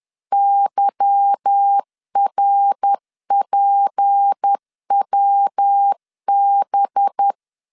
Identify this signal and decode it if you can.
Morse code, heard as a keyed tone, about 10 wpm, 800 Hz. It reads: YRPWB
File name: LEZIONI / 8 YJ45 / 8 Y 10wpm 2615